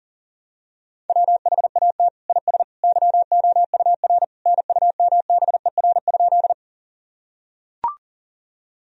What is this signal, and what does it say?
Morse code, heard as a keyed tone, about 37 wpm, 700 Hz. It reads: WHAT IS YOUR NUMBER?
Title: Sentences from Top 1000 Words - Review Part 1 - 40wpm